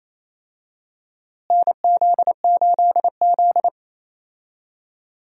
Morse code, heard as a keyed tone, about 28 wpm, 700 Hz. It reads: NZ8Z
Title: USA Call Signs - 4 Characters - Repeated 2x - 28wpm